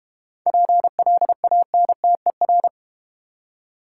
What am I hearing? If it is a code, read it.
Morse code, heard as a keyed tone, about 32 wpm, 700 Hz. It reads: PLANTER